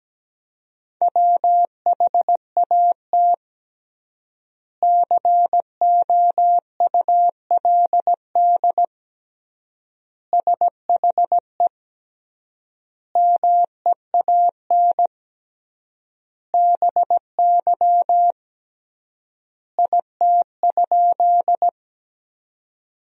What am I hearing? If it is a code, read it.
Morse code, heard as a keyed tone, about 17 wpm, 700 Hz. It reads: WHAT COULD SHE MEAN BY IT?
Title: 3x Word-Spacing - Sentences from Top 200 Words - 17wpm